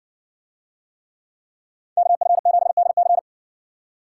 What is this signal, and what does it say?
Morse code, heard as a keyed tone, about 54 wpm, 700 Hz. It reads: KF7DC